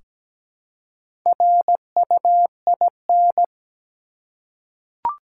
Morse code, heard as a keyed tone, about 17 wpm, 700 Hz. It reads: RUIN
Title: Four Letter Words - 17wpm